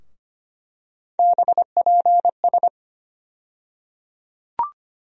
Morse code, heard as a keyed tone, about 25 wpm, 700 Hz. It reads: BPS
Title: USA Call Sign Postfixes - 3 Letters 25wpm